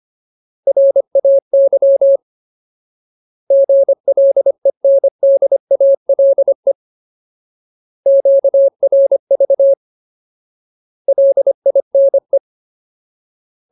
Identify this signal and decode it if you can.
Morse code, heard as a keyed tone, about 25 wpm, 550 Hz. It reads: RAY GLENDALE QRV LINE
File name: Ultimate Word List / 25WPM Ultimate